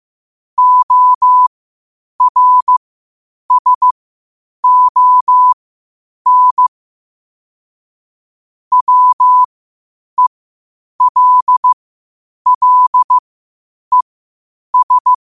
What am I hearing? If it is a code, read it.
Morse code, heard as a keyed tone, about 15 wpm, spaced as for 8 wpm, 1000 Hz. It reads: ORSON WELLES
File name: qotdfiles / 8wpm